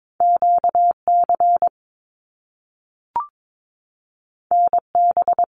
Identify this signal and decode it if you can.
Morse code, heard as a keyed tone, about 22 wpm, 700 Hz. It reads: QC NB
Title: Canadian Provinces and Territories 22wpm